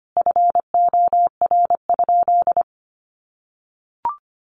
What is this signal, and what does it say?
Morse code, heard as a keyed tone, about 25 wpm, 700 Hz. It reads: FOR?
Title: Sentences from Top 1200 Words 25wpm